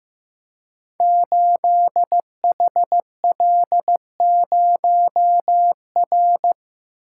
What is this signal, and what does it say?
Morse code, heard as a keyed tone, about 15 wpm, 700 Hz. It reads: 8HL0R
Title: Code Groups - 5 Letter:Numbers - 15wpm